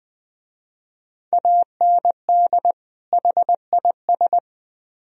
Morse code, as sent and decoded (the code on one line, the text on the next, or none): .- -. -.. / .... .. ...
AND HIS